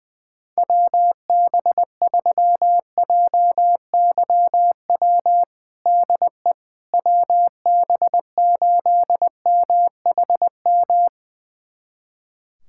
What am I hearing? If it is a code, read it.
Morse code, heard as a keyed tone, about 20 wpm, 700 Hz. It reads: WB3JYW DE WB8MHM
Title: USA Callsign DE Callsign - 6 Characters - 20wpm